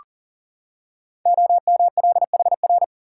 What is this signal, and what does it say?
Morse code, heard as a keyed tone, about 40 wpm, 690 Hz. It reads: OMLHR
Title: Code Groups - 5 Letters - 40wpm